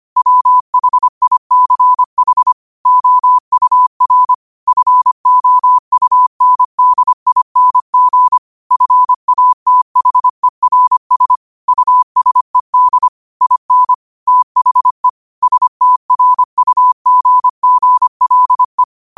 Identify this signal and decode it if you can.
Morse code, heard as a keyed tone, about 25 wpm, 1000 Hz. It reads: WHICH OUR FOUNDING FATHERS USED IN THE STRUGGLE
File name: qotdfiles / 25wpm